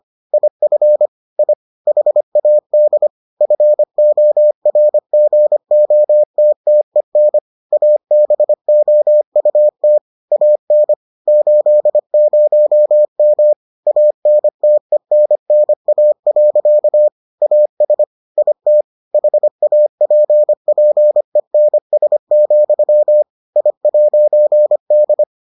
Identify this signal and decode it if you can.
Morse code, heard as a keyed tone, about 25 wpm, 600 Hz. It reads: IF I HAD FORGOTTEN ABOUT AN 80M ANTENNA. AS IT HAPPENS, I'D